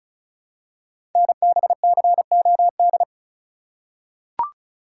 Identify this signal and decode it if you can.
Morse code, heard as a keyed tone, about 36 wpm, 700 Hz. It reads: NBCOD
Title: Code Groups - 5 Letters - 35wpm